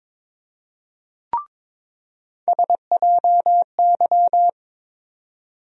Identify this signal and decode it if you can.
Morse code, heard as a keyed tone, about 22 wpm, 700 Hz. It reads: SJY
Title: USA Call Sign Postfixes - 3 Letters 22wpm